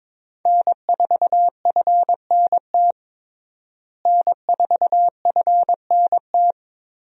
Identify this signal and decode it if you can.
Morse code, heard as a keyed tone, about 22 wpm, 700 Hz. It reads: N4FNT N4FNT
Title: USA Call Signs - 5 Characters - Repeated 2x - 22wpm